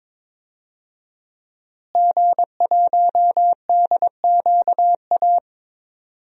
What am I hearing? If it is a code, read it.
Morse code, heard as a keyed tone, about 22 wpm, 700 Hz. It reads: G1DQA
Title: DXCC-100 Call Signs - 5 Characters - Repeated 2x - 22wpm